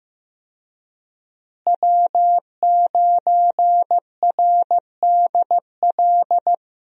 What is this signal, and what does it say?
Morse code, heard as a keyed tone, about 15 wpm, 700 Hz. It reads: W9RDL
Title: USA Call Signs - 5 Characters - Repeated 2x - 15wpm